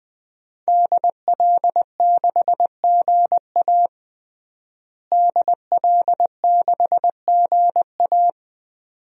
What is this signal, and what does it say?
Morse code, heard as a keyed tone, about 20 wpm, 700 Hz. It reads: DL6GA DL6GA
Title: DXCC-100 Call Signs - 5 Characters - Repeated 2x - 20wpm